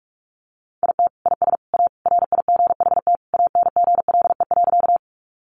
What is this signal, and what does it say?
Morse code, heard as a keyed tone, about 44 wpm, 700 Hz. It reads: IT IS A RIGHT ANGLE.
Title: Sentences from Top 1500 Words 45wpm